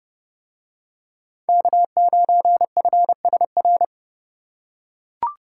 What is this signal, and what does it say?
Morse code, heard as a keyed tone, about 27 wpm, 700 Hz. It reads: K9FSR